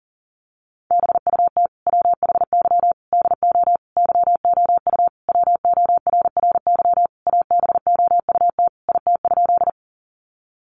Morse code, heard as a keyed tone, about 40 wpm, 700 Hz. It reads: BUT WHY DO YOU WORRY ABOUT IT?
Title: Sentences from Top 1100 Words 40wpm